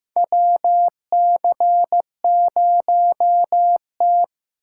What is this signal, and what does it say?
Morse code, heard as a keyed tone, about 15 wpm, 700 Hz. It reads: WC0T
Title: USA Call Signs - 4 Characters - 3x Character Spacing - 15wpm